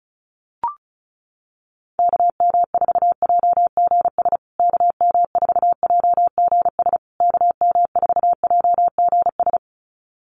Morse code, heard as a keyed tone, about 34 wpm, 700 Hz. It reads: KM4JGS KM4JGS KM4JGS
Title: USA Call Signs Repeated 3x 35wpm